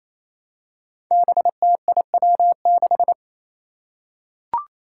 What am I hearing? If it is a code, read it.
Morse code, heard as a keyed tone, about 28 wpm, 700 Hz. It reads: BTIW6
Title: Five Challenging Letter-Number Combinations 28wpm